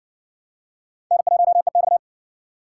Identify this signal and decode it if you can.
Morse code, heard as a keyed tone, about 55 wpm, 700 Hz. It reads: N1EX